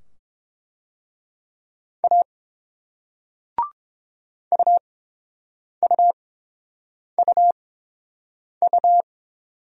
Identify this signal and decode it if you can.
Morse code, heard as a keyed tone, about 30 wpm, 690 Hz. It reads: A U U U U